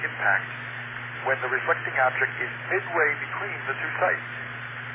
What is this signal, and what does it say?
Emergency Alert System (SAME) tones, heard as harsh buzzing bursts, with speech. It is heard from 0.6 to 3.9 s.